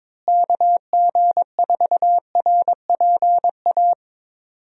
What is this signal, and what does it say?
Morse code, heard as a keyed tone, about 22 wpm, 700 Hz. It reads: KG4RPA
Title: USA Call Signs - 6 Characters - Repeated 3x - 22wpm